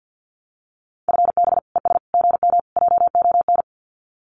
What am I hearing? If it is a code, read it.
Morse code, heard as a keyed tone, about 50 wpm, 700 Hz. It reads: FB ES GM JON
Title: Common QSO Phrases Encore 50wpm